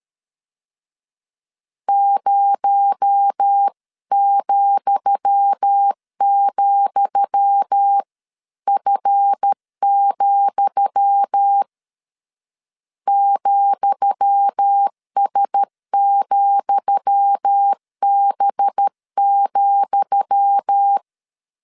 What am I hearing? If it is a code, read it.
Morse code, heard as a keyed tone, about 13 wpm, 800 Hz. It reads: 0,,F, ,S,B,
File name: LEZIONI / 10 SIGNS / 10D COMA 12wpm 2544